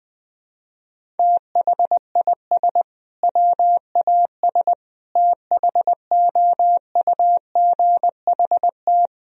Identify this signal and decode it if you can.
Morse code, heard as a keyed tone, about 20 wpm, 700 Hz. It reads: THIS WAS THOUGHT
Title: Sentences from Top 1000 Words - Review Part 1 - 20wpm